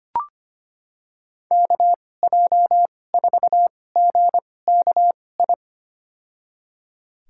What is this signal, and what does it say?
Morse code, heard as a keyed tone, about 25 wpm, spaced as for 18 wpm, 700 Hz. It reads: KJ4GKI
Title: USA Call Signs - 6 Characters - 2x Character Spacing - 25wpm